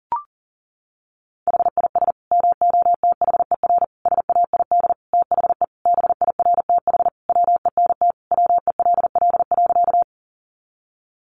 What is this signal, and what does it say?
Morse code, heard as a keyed tone, about 37 wpm, 700 Hz. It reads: HIS MOTHER SAID THE BIRTH WENT WELL.
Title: Sentences from Top 800 Words 40wpm